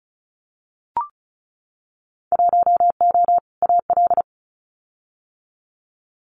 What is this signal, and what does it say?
Morse code, heard as a keyed tone, about 36 wpm, 700 Hz. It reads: 1O AL
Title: Winter Field Day Exchanges 35wpm